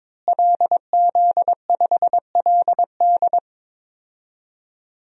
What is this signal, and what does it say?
Morse code, heard as a keyed tone, about 22 wpm, 700 Hz. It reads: LZ5LD
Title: DXCC-100 Call Signs - 5 Characters - Repeated 2x - 22wpm